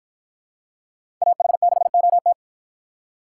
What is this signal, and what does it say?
Morse code, heard as a keyed tone, about 53 wpm, 700 Hz. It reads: AH6OT